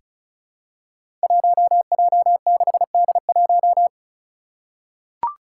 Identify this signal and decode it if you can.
Morse code, heard as a keyed tone, about 36 wpm, 700 Hz. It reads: 1J6D1
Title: Five Challenging Letter-Number Combinations 35wpm